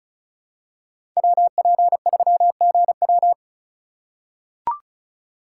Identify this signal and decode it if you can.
Morse code, heard as a keyed tone, about 34 wpm, 700 Hz. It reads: WP3GW
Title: USA Call Signs - 5 Characters - 2x Character Spacing - 35wpm